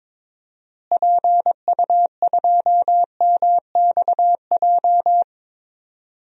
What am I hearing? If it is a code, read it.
Morse code, heard as a keyed tone, about 22 wpm, 700 Hz. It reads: PU2MXJ
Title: DXCC-100 Call Signs - 6 Characters - Repeated 2x - 22wpm